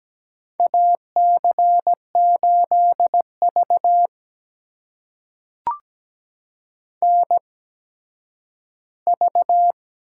Morse code, heard as a keyed tone, about 17 wpm, spaced as for 10 wpm, 700 Hz. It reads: AC8V NV
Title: USA Call Signs - 4 Characters - 8x Character Spacing - 17wpm